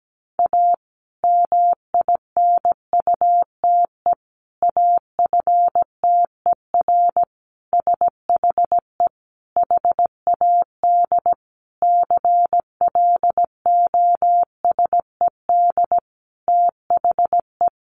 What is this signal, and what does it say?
Morse code, heard as a keyed tone, about 17 wpm, 700 Hz. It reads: A MINUTE AFTER SHE HAD CLOSED THE
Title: Sentences from Top 1400 Words 17wpm